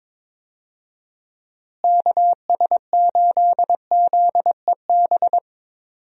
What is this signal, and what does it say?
Morse code, heard as a keyed tone, about 22 wpm, 700 Hz. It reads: KS8ZEB
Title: USA Call Signs - 6 Characters - Repeated 3x - 22wpm